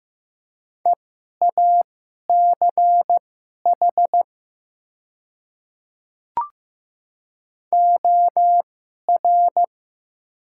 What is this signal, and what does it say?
Morse code, heard as a keyed tone, about 15 wpm, spaced as for 11 wpm, 700 Hz. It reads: EACH OR
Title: 2x Character Spacing - Top 100 Words - 15wpm